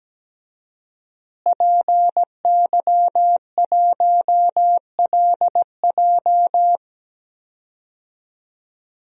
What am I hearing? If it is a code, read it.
Morse code, heard as a keyed tone, about 17 wpm, 700 Hz. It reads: PY1LJ
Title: DXCC-100 Call Signs - 5 Characters - Repeated 2x - 17wpm